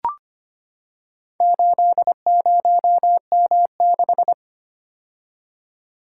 Morse code, heard as a keyed tone, about 25 wpm, 700 Hz. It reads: 80M6
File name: practice / 25wpm / Four Letter-Number Combinations - 25wpm